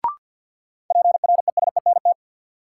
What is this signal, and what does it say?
Morse code, heard as a keyed tone, about 52 wpm, 700 Hz. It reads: PRESENT
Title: Sentences from Top 800 Words - New Word Drill - 50wpm